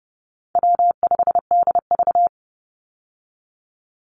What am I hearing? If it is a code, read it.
Morse code, heard as a keyed tone, about 30 wpm, 700 Hz. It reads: W5DV
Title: Easy Call Signs Repeated 3x - 30wpm